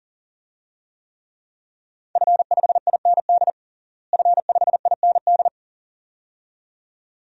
Morse code, heard as a keyed tone, about 40 wpm, 690 Hz. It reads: F5IND F5IND
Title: DXCC-100 Call Signs - 5 Characters - Repeated 2x - 40wpm